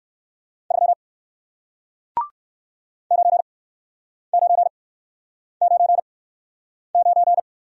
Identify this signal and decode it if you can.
Morse code, heard as a keyed tone, about 60 wpm, spaced as for 9 wpm, 700 Hz. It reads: 3 9999